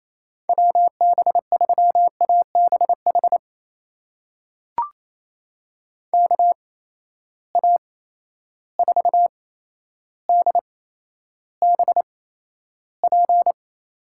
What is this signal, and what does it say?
Morse code, heard as a keyed tone, about 28 wpm, spaced as for 12 wpm, 700 Hz. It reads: WB3ABH KA4DBP
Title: USA Call Signs - 6 Characters - 8x Character Spacing - 28wpm